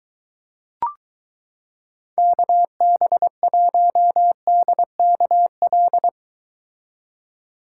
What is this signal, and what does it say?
Morse code, heard as a keyed tone, about 23 wpm, 700 Hz. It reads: KB1DKL